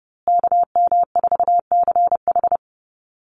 Morse code, heard as a keyed tone, about 30 wpm, 700 Hz. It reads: KM4CH
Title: Call Signs Repeated 2x - 30wpm